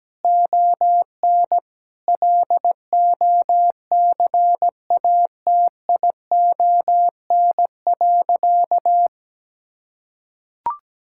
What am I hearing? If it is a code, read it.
Morse code, heard as a keyed tone, about 17 wpm, 700 Hz. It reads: ON LOCATION.
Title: Sentences from Top 1000 Words - Review Part 6 - 17wpm